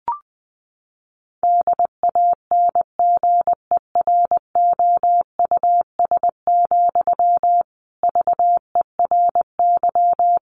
Morse code, heard as a keyed tone, about 20 wpm, 700 Hz. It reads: DANGEROUS, VERY
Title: Sentences from Top 1200 Words 20wpm